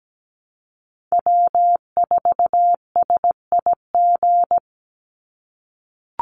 Morse code, heard as a keyed tone, about 17 wpm, 700 Hz. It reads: W4SIG
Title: Call Signs - 17wpm